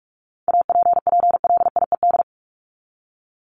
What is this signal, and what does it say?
Morse code, heard as a keyed tone, about 44 wpm, 700 Hz. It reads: APPLIED
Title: ED Suffix - Top 100 Words with Base Words 45wpm